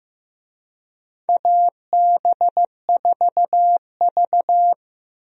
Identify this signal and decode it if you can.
Morse code, heard as a keyed tone, about 15 wpm, 700 Hz. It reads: AB4V
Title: USA Call Signs - 4 Characters - 15wpm